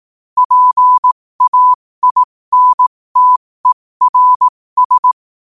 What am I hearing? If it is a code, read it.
Morse code, heard as a keyed tone, about 18 wpm, spaced as for 15 wpm, 1000 Hz. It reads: PAINTERS
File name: qotdfiles / 15wpm